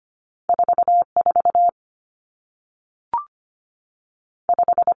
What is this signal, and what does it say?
Morse code, heard as a keyed tone, about 25 wpm, 700 Hz. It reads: 4 4 5
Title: Common Cut Numbers - 2-Digit 25wpm